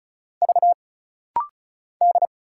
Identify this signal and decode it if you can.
Morse code, heard as a keyed tone, about 34 wpm, 700 Hz. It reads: V D
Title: Single Letters - Rapid-Fire 35wpm